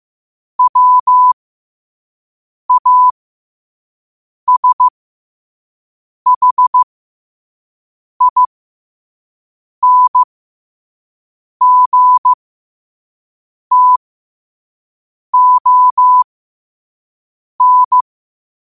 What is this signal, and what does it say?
Morse code, heard as a keyed tone, about 15 wpm, spaced as for 5 wpm, 1000 Hz. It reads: WASHINGTON
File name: qotdfiles / 5wpm